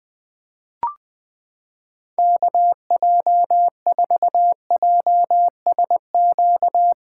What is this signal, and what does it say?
Morse code, heard as a keyed tone, about 20 wpm, 700 Hz. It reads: KJ4JSQ
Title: USA Callsign DE Callsign 20wpm